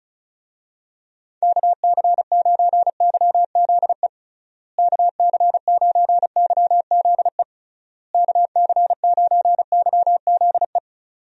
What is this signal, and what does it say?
Morse code, heard as a keyed tone, about 36 wpm, 700 Hz. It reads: KC9YZE KC9YZE KC9YZE
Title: USA Call Signs - 6 Characters - Repeated 3x - 35wpm